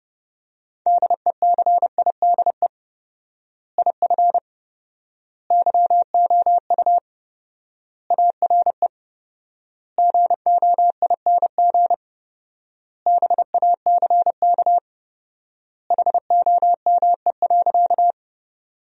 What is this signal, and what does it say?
Morse code, heard as a keyed tone, about 32 wpm, 700 Hz. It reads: DECIDE IF YOU ARE GOING BACK HOME.
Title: 4x Word-Spacing - Sentences from Top 500 Words - 30wpm